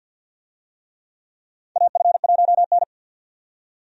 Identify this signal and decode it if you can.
Morse code, heard as a keyed tone, about 48 wpm, 700 Hz. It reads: AF1N
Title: USA Call Signs - 4 Characters - 50wpm